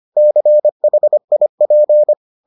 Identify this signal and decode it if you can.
Morse code, heard as a keyed tone, about 25 wpm, 600 Hz. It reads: CHIP